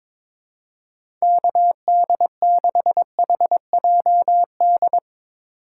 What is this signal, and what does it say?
Morse code, heard as a keyed tone, about 22 wpm, 700 Hz. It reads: KD6HJD
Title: USA Call Signs - 6 Characters - 4x Character Spacing - 22wpm